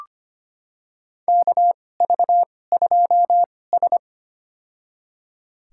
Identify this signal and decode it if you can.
Morse code, heard as a keyed tone, about 25 wpm, spaced as for 18 wpm, 700 Hz. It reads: KV2S
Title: USA Call Signs - 4 Characters - 2x Character Spacing - 25wpm